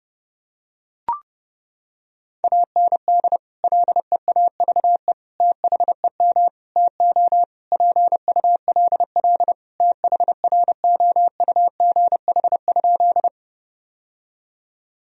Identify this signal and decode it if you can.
Morse code, heard as a keyed tone, about 30 wpm, 700 Hz. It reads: AND LEAVE THEM TO PULL THROUGH?
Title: Sentences from Top 1000 Words - Review Part 2 - 30wpm